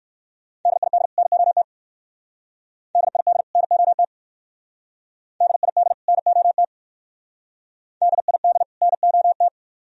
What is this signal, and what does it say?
Morse code, heard as a keyed tone, about 56 wpm, 690 Hz. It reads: DID NOT DID NOT DID NOT DID NOT
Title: Speed-Racing - Sets of 2 Words 45wpm